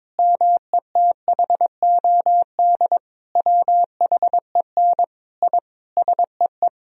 Morse code, heard as a keyed tone, about 22 wpm, 700 Hz. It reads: METHOD WHEN I SEE
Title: Sentences from Top 1000 Words - Review Part 3 - 22wpm